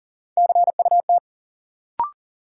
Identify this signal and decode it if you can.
Morse code, heard as a keyed tone, about 38 wpm, 700 Hz. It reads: CUT
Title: Sentences from Top 300 Words - New Word Drill - 40wpm